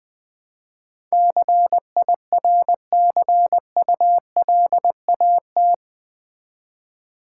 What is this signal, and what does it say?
Morse code, heard as a keyed tone, about 20 wpm, 700 Hz. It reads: CIRCULAT
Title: Build A Word - 4 Letters At A Time - 11-Letter Words - 20wpm